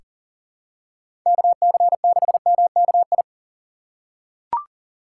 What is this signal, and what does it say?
Morse code, heard as a keyed tone, about 37 wpm, 700 Hz. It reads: KC6MKI
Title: USA Call Signs - 6 Characters - 4x Character Spacing - 40wpm